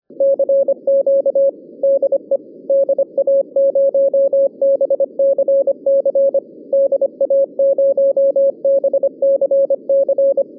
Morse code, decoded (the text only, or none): CQ DE DA0BCC DA0BCC